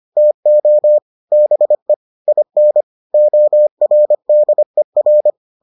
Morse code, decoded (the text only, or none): TO BE IN ORDER